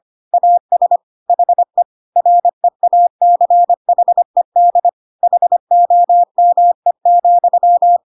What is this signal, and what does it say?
Morse code, heard as a keyed tone, about 25 wpm, 700 Hz. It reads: AS HE REACHED HOME,
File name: Pinocchio / Pinocchio ch3 25 25 ik1hge